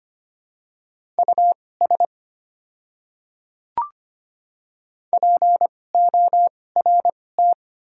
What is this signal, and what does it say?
Morse code, heard as a keyed tone, about 25 wpm, spaced as for 18 wpm, 700 Hz. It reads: US PORT
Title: 2x Character Spacing - Top 200 Words - 25wpm